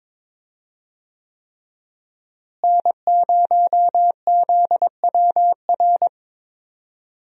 Morse code, heard as a keyed tone, about 22 wpm, 700 Hz. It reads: N0ZWR
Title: USA Call Signs - 5 Characters - Repeated 3x - 22wpm